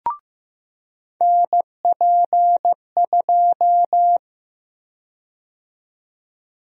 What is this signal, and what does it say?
Morse code, heard as a keyed tone, about 15 wpm, 700 Hz. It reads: NP2
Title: USA Call Sign Prefixes - 15wpm